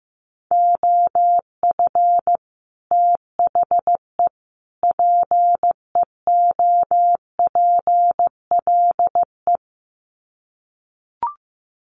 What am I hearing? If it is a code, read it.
Morse code, heard as a keyed tone, about 15 wpm, 700 Hz. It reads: OF THE PEOPLE
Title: Sets of 3 Words 15wpm